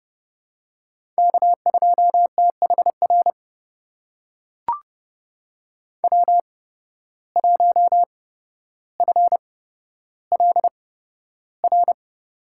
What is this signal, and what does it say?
Morse code, heard as a keyed tone, about 31 wpm, spaced as for 13 wpm, 700 Hz. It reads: K2THR W1FLR
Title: USA Call Signs - 5 Characters - 8x Character Spacing - 30wpm